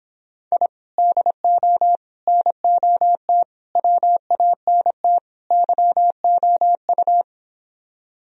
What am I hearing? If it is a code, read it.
Morse code, heard as a keyed tone, about 26 wpm, 700 Hz. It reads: I DO NOT WANT YOU